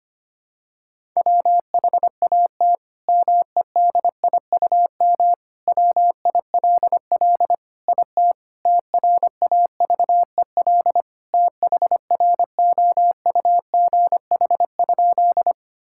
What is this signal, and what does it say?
Morse code, heard as a keyed tone, about 25 wpm, 700 Hz. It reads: WHAT MEDIUM WILL IT TRAVEL THROUGH?
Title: Sentences from Top 1000 Words - Review Part 6 - 25wpm